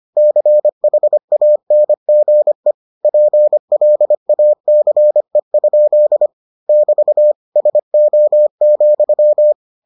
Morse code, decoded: CHANGE PLACE? <BT> SO,

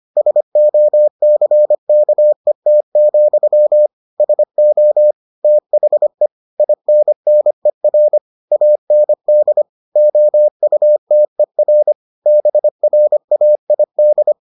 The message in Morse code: SOCKET, SO THE INNER AND OUTER BRAID

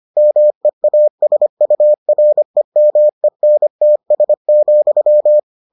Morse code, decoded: MEASUREMENTS,

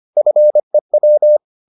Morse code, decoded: FEW

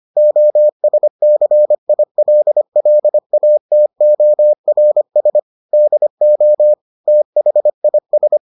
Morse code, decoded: OSCILLATORS DO THIS

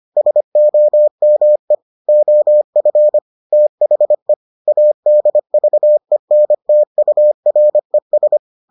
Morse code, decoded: SOME OF THE ADVENTURES

